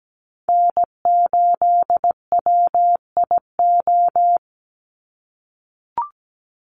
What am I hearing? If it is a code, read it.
Morse code, heard as a keyed tone, about 17 wpm, 700 Hz. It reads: N8WIO
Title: USA Call Signs 17wpm